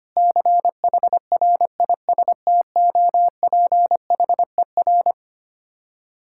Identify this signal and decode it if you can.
Morse code, heard as a keyed tone, about 25 wpm, 700 Hz. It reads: CHRISTOPHER
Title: Practice QSO - Exchange 1 - Beginning - 25wpm